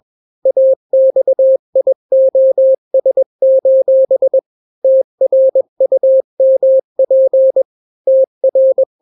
Morse code, heard as a keyed tone, about 21 wpm, 530 Hz. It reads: AXIOS: TRUMP TR
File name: daily / 20wpm